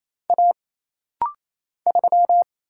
Morse code, heard as a keyed tone, about 28 wpm, 700 Hz. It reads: A 3
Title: Single Letter-Number - Rapid-Fire 28wpm